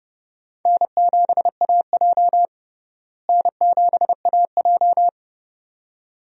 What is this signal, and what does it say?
Morse code, heard as a keyed tone, about 30 wpm, 700 Hz. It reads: N7AJ N7AJ
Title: USA Call Signs - 4 Characters - Repeated 2x - 30wpm